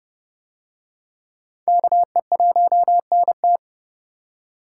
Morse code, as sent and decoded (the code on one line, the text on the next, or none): -.- . .---- -. -
KE1NT